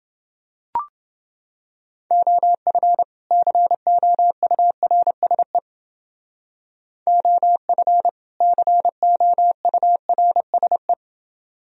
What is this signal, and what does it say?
Morse code, heard as a keyed tone, about 28 wpm, 700 Hz. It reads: OF COURSE OF COURSE